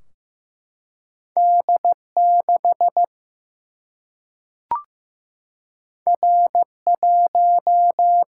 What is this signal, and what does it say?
Morse code, heard as a keyed tone, about 15 wpm, 700 Hz. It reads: D 6 R 1
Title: Single-Letter Single-Number 15wpm